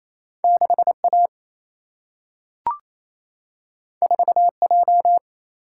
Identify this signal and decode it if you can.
Morse code, heard as a keyed tone, about 28 wpm, 700 Hz. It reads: 6 A 4 J
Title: Single-Letter Single-Number 28wpm